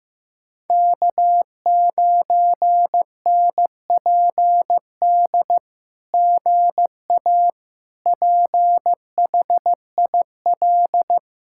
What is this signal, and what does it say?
Morse code, heard as a keyed tone, about 15 wpm, 700 Hz. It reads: K9NPD GA PHIL
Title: SOTA QSO Practice 15wpm